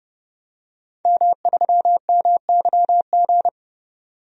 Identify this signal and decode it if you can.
Morse code, heard as a keyed tone, about 30 wpm, 700 Hz. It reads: M3MYG
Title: DXCC-100 Call Signs - 5 Characters - 30wpm